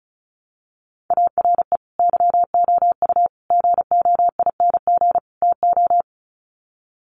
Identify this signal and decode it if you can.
Morse code, heard as a keyed tone, about 34 wpm, 700 Hz. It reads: ARE YOU GOING TO
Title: Sets of 4 Words 35wpm